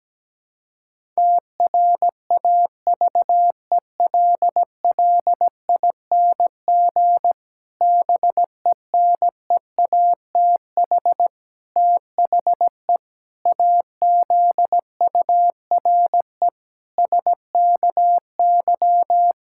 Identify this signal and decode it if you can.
Morse code, heard as a keyed tone, about 17 wpm, 700 Hz. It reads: TRAVELLING BENEATH THE AZURE SKY
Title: Pangram Challenge v2 - 17wpm